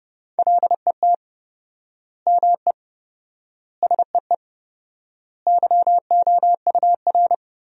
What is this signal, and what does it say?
Morse code, heard as a keyed tone, about 32 wpm, 700 Hz. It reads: LET ME SEE YOUR
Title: 4x Word-Spacing - Sentences from Top 500 Words - 30wpm